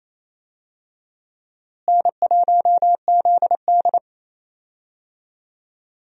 Morse code, heard as a keyed tone, about 28 wpm, 700 Hz. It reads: N1ZD
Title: USA Call Signs - 4 Characters - Repeated 2x - 28wpm